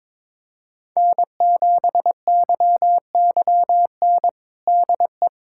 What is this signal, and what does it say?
Morse code, heard as a keyed tone, about 21 wpm, 700 Hz. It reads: N7YYN DE
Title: USA Callsign DE Callsign - 5 Characters - 22wpm